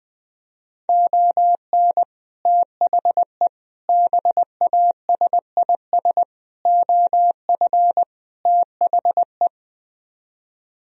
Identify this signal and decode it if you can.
Morse code, heard as a keyed tone, about 21 wpm, 700 Hz. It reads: ON THE BASIS OF THE
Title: Speed-Racing - Sets of 5 Words 20wpm